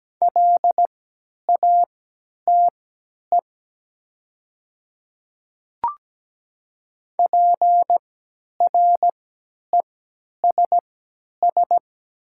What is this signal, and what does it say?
Morse code, heard as a keyed tone, about 17 wpm, spaced as for 10 wpm, 700 Hz. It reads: LATE PRESS
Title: 3x Character Spacing - Top 300 Words - 17wpm